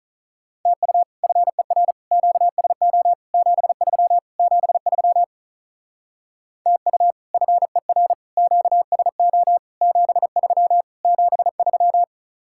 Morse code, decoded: TU FER QSO 73 73 TU FER QSO 73 73

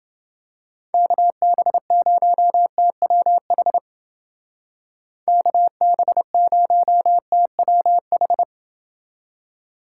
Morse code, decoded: KB0TWH KB0TWH

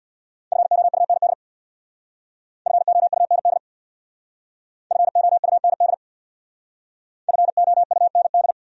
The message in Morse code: FOUND FOUND FOUND FOUND

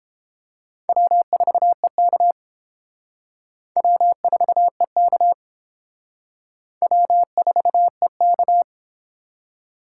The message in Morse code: W4EK W4EK W4EK

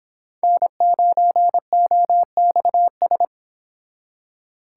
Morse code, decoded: N9OXS

